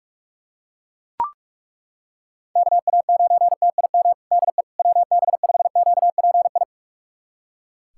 Morse code, heard as a keyed tone, about 42 wpm, 700 Hz. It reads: KA9TIM DE WB5QPI
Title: USA Callsign DE Callsign - 6 Characters - 45wpm